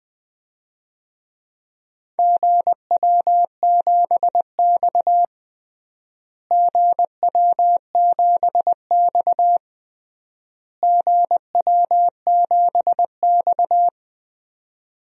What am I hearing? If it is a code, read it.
Morse code, heard as a keyed tone, about 20 wpm, 700 Hz. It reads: GW7X GW7X GW7X
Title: DXCC-100 Call Signs - 4 Characters - Repeated 3x - 20wpm